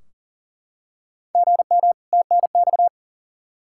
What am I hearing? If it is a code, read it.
Morse code, heard as a keyed tone, about 40 wpm, 700 Hz. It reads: GM TNX
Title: Practice QSO - Exchange 1 - Beginning - 40wpm